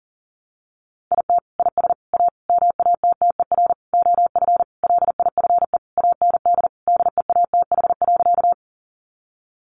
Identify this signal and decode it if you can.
Morse code, heard as a keyed tone, about 40 wpm, 700 Hz. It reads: IT IS A MATTER OF LIFE AND DEATH.
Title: Sentences from Top 1100 Words 40wpm